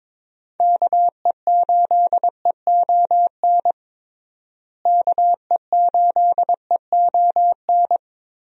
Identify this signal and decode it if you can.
Morse code, heard as a keyed tone, about 22 wpm, 700 Hz. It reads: KE8EON KE8EON
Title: USA Call Signs - 6 Characters - Repeated 2x - 22wpm